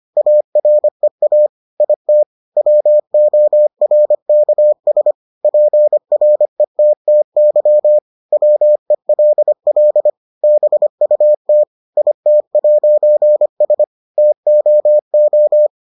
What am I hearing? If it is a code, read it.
Morse code, heard as a keyed tone, about 25 wpm, 600 Hz. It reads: AREA IT WORKS PRETTY WELL BUT IT'S TOO